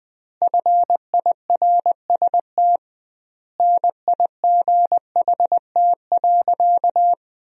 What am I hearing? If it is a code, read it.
Morse code, heard as a keyed tone, about 19 wpm, 700 Hz. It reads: FIRST NIGHT.
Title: 2x Word-Spacing - Sentences from Top 300 Words - 20wpm